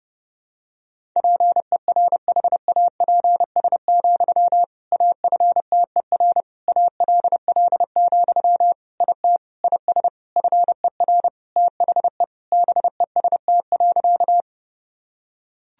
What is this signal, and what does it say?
Morse code, heard as a keyed tone, about 30 wpm, 700 Hz. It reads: PERHAPS, AFTER ALL, IT IS FER THE BEST.
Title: Sentences from Top 500 Words and Abbreviations 30wpm